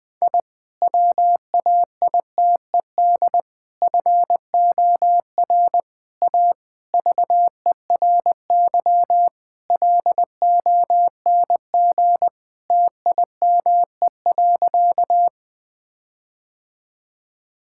Morse code, heard as a keyed tone, about 20 wpm, 700 Hz. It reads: I WAITED FOR A VERY LONG TIME.
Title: Sentences with 5-Word Phrases - Spoken Beginning - 20wpm